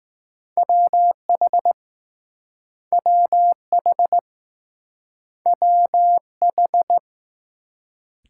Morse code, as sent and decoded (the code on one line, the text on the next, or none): .-- .... / .-- .... / .-- ....
WH WH WH